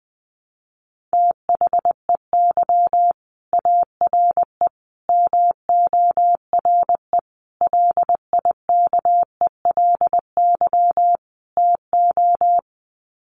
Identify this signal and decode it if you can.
Morse code, heard as a keyed tone, about 20 wpm, 700 Hz. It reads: THEY ARE MORE LIKELY TO
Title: Sets of 5 Words Encore2 20wpm